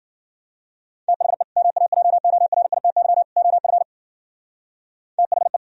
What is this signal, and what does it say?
Morse code, heard as a keyed tone, about 51 wpm, 700 Hz. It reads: THE MAJORITY OF THE